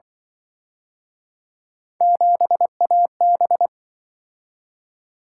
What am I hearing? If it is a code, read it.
Morse code, heard as a keyed tone, about 25 wpm, 700 Hz. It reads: 7AB
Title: Speed-Racing - Three Letter-Number Combinations - 20wpm